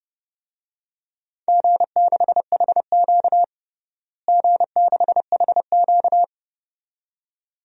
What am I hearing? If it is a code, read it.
Morse code, heard as a keyed tone, about 32 wpm, 700 Hz. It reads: G6HQ G6HQ
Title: DXCC-100 Call Signs - 4 Characters - Repeated 2x - 30wpm